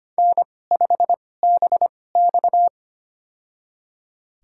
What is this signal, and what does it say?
Morse code, heard as a keyed tone, about 25 wpm, spaced as for 18 wpm, 700 Hz. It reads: N5BX
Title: USA Call Signs - 4 Characters - 2x Character Spacing - 25wpm